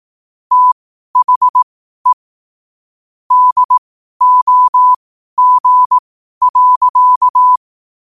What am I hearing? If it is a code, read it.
Morse code, heard as a keyed tone, about 18 wpm, spaced as for 12 wpm, 1000 Hz. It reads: THE DOG.
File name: qotdfiles / 12wpm